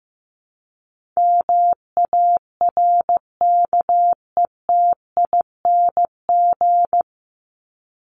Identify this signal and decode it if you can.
Morse code, heard as a keyed tone, about 15 wpm, 700 Hz. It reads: MARKETING
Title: Top 751-1000 Words 15wpm